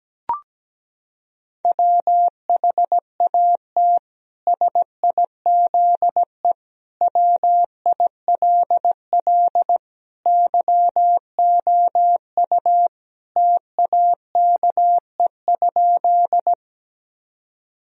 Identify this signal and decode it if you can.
Morse code, heard as a keyed tone, about 17 wpm, 700 Hz. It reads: WHAT SIZE WILL YOU TAKE?
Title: Sentences from Top 1000 Words - Review Part 2 - 17wpm